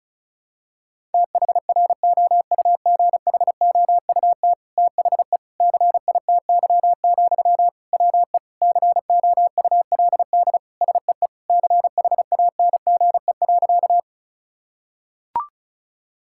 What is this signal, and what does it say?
Morse code, heard as a keyed tone, about 36 wpm, 700 Hz. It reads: THROUGHOUT THE CITY, WE COULD SEE CHANGE.
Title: Sentences from Top 1000 Words - Review Part 5 - 35wpm